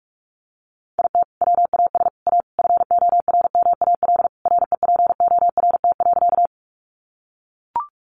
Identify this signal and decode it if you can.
Morse code, heard as a keyed tone, about 44 wpm, 700 Hz. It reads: IT WAS A FORMAL REPORT.
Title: Sentences from Top 1300 Words 45wpm